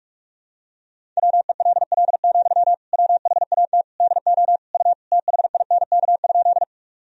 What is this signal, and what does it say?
Morse code, heard as a keyed tone, about 46 wpm, 700 Hz. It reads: WELL, WHAT DO U THINK?
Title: Sentences from Top 200 Words and Abbreviations 45wpm